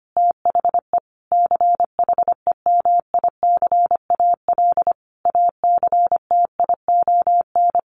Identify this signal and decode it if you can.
Morse code, heard as a keyed tone, about 25 wpm, 700 Hz. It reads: THE CHEMICAL ACTION
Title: Sentences from Top 1200 Words 25wpm